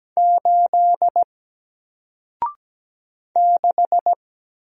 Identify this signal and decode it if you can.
Morse code, heard as a keyed tone, about 17 wpm, 700 Hz. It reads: 8 6
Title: Single Letter-Number Alt 17wpm